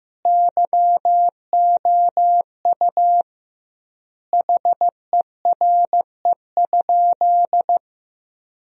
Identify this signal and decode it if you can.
Morse code, heard as a keyed tone, about 15 wpm, 700 Hz. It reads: YOU HERE?
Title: 2x Word-Spacing - Sentences from Top 200 Words - 15wpm